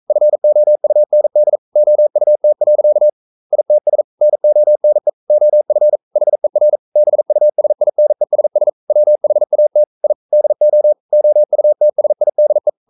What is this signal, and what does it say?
Morse code, heard as a keyed tone, about 42 wpm, 600 Hz. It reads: FOUND OUT. ITS NONE OF HER BUSINESS WHAT I DO OUTSIDE
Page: Romanen Never af Ken Follet på engelsk.